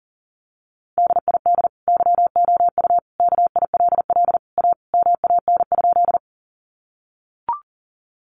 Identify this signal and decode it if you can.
Morse code, heard as a keyed tone, about 40 wpm, 700 Hz. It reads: DID YOU KILL A MAN?
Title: Sentences from Top 1400 Words 40wpm